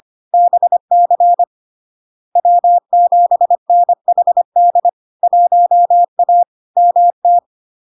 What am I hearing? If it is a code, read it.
Morse code, heard as a keyed tone, about 25 wpm, 700 Hz. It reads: BC W7NHD 1A MT
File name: Log-in-CW-25wpm-700-Hz